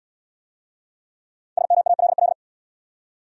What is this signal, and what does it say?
Morse code, heard as a keyed tone, about 64 wpm, 700 Hz. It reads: SMALL